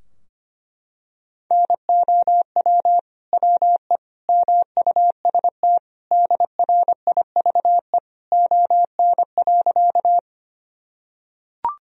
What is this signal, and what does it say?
Morse code, heard as a keyed tone, about 25 wpm, 700 Hz. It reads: NOW WE MUST DRIVE ON.
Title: Sentences from Top 1000 Words - Review Part 2 - 25wpm